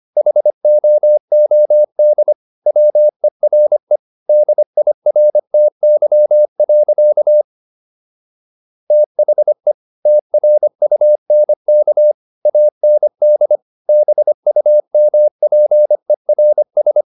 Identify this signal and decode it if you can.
Morse code, heard as a keyed tone, about 25 wpm, 600 Hz. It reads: HOOD WERE DIRTY. THE TRUNK AND BUMPERS